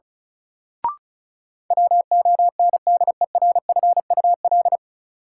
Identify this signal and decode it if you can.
Morse code, heard as a keyed tone, about 34 wpm, 700 Hz. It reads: WONDERFUL
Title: Sentences from Top 1100 Words - New Word Drill Alt - 35wpm